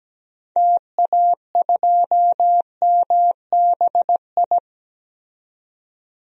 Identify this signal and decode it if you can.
Morse code, heard as a keyed tone, about 17 wpm, 700 Hz. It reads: TA2MBI
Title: DXCC-100 Call Signs - 6 Characters - 17wpm